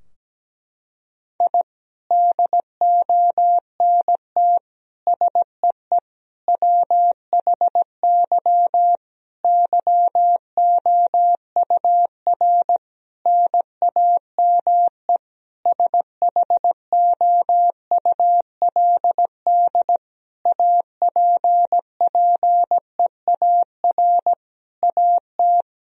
Morse code, heard as a keyed tone, about 17 wpm, 700 Hz. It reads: I DONT SEE WHY YOUR NAME SHOULD APPEAR AT
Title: Sentences from Top 1000 Words - Review Part 2 - 17wpm